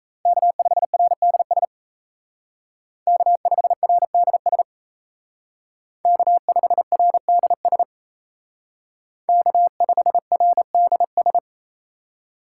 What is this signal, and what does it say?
Morse code, heard as a keyed tone, about 35 wpm, 700 Hz. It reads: K5RDS K5RDS K5RDS K5RDS